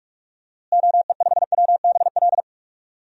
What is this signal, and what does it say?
Morse code, heard as a keyed tone, about 48 wpm, 700 Hz. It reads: OE5WBL
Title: DXCC-100 Call Signs - 6 Characters - Repeated 2x - 45wpm